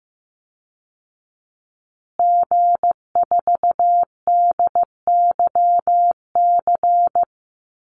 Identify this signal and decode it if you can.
Morse code, heard as a keyed tone, about 15 wpm, 700 Hz. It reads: G4DYC
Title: Call Signs Repeated 3x - 15wpm